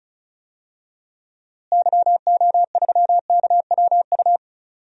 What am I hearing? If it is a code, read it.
Morse code, heard as a keyed tone, about 36 wpm, 700 Hz. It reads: YO3KWU
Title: DXCC-100 Call Signs - 6 Characters - Repeated 2x - 35wpm